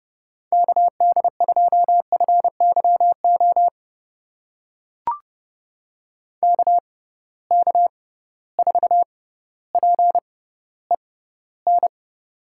Morse code, heard as a keyed tone, about 30 wpm, spaced as for 15 wpm, 700 Hz. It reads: KD2FYO KK4PEN
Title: USA Call Signs - 6 Characters - 6x Character Spacing - 30wpm